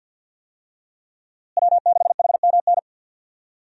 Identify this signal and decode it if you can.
Morse code, heard as a keyed tone, about 51 wpm, 700 Hz. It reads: W6HMN
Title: USA Call Signs - 5 Characters - Repeated 3x - 50wpm